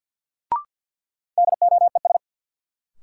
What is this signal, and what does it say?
Morse code, heard as a keyed tone, about 52 wpm, 700 Hz. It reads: DOES
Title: Sentences from Top 200 Words - New Word Drill - 50wpm